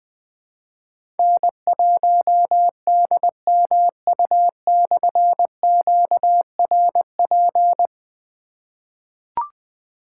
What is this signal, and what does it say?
Morse code, heard as a keyed tone, about 20 wpm, 700 Hz. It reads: N1DMU/QRP
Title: Difficult Call Signs 20wpm